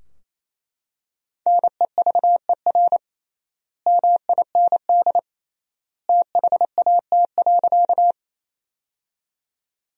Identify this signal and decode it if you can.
Morse code, heard as a keyed tone, about 28 wpm, 700 Hz. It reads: NEVER MIND THAT.
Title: 3x Word-Spacing - Sentences from Top 500 Words - 28wpm